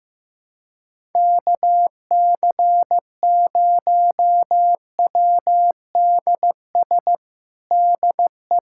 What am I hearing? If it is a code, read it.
Morse code, heard as a keyed tone, about 15 wpm, 700 Hz. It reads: KC0WDS DE
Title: USA Callsign DE Callsign - 6 Characters - 15wpm